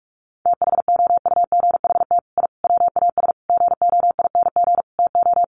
Morse code, heard as a keyed tone, about 44 wpm, 700 Hz. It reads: THOUGHT I WAS GOING TO
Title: Sets of 5 Words Encore2 45wpm